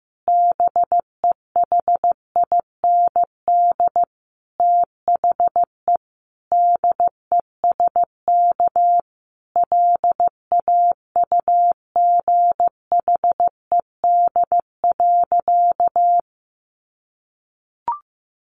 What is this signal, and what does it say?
Morse code, heard as a keyed tone, about 14 wpm, 700 Hz. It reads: BEHIND THE DESK LAUGHED.
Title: A Sound Of Thunder 15wpm